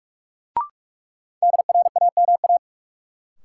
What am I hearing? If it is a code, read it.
Morse code, heard as a keyed tone, about 46 wpm, 700 Hz. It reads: DRAMA
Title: Sentences from Top 1500 Words - New Word Drill Alt - 45wpm